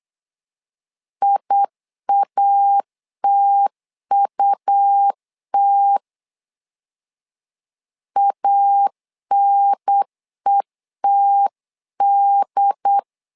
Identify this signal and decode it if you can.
Morse code, heard as a keyed tone, about 8 wpm, 800 Hz. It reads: IATUT ANETD
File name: LEZIONI / 3 MWRO / 3 M 8wpm 2300